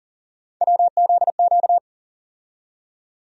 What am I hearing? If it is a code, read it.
Morse code, heard as a keyed tone, about 40 wpm, 700 Hz. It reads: WZQ
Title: USA Call Sign Postfixes - 3 Letters 40wpm